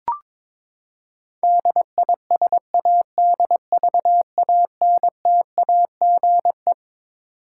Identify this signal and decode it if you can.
Morse code, heard as a keyed tone, about 22 wpm, 700 Hz. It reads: DISADVANTAGE
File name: Twelve Letter Words - 22wpm